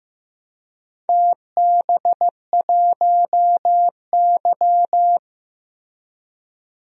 Morse code, heard as a keyed tone, about 15 wpm, 700 Hz. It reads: TB1Y
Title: DXCC-100 Call Signs - 4 Characters - Repeated 3x - 15wpm